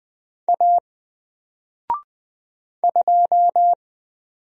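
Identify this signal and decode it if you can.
Morse code, heard as a keyed tone, about 20 wpm, 700 Hz. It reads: A 2
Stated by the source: Single letter and number Morse drill at 20 wpm, from N4LCD.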